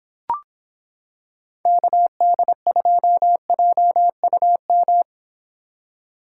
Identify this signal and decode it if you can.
Morse code, heard as a keyed tone, about 26 wpm, 700 Hz. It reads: KD2JUM